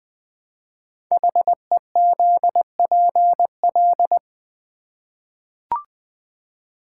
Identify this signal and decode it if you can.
Morse code, heard as a keyed tone, about 20 wpm, 700 Hz. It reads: HEZPL
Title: Code Groups - 5 Letters - 20wpm